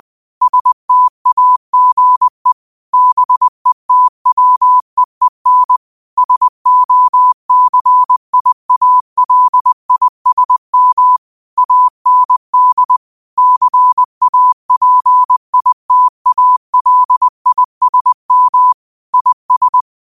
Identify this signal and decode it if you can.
Morse code, heard as a keyed tone, about 20 wpm, 1000 Hz. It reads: STAGE BETWEEN SOCIALISM AND CAPITALISM IS
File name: qotdfiles / 20wpm